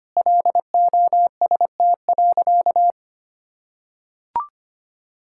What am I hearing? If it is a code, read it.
Morse code, heard as a keyed tone, about 25 wpm, 700 Hz. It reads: LOST.
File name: practice / 25wpm / Sentences with 2-Word Phrases - 25wpm